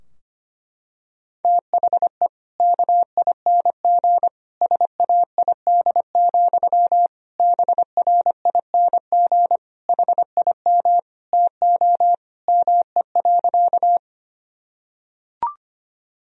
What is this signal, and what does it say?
Morse code, heard as a keyed tone, about 25 wpm, 700 Hz. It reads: THE KING SAID, BRING HIM TO ME.
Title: Sentences from Top 1000 Words - Review Part 3 - 25wpm